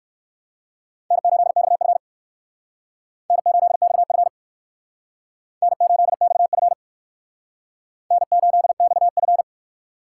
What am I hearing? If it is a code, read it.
Morse code, heard as a keyed tone, about 59 wpm, 700 Hz. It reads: N8XF N8XF N8XF N8XF